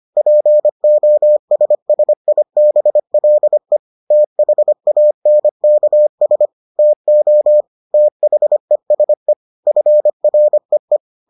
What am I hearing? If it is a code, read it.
Morse code, heard as a keyed tone, about 25 wpm, 600 Hz. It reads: POSSIBLE THANKS TO THESE FREE